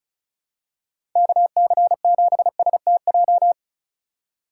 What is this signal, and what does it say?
Morse code, heard as a keyed tone, about 34 wpm, 700 Hz. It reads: KC7STJ
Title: USA Call Signs - 6 Characters - 2x Character Spacing - 35wpm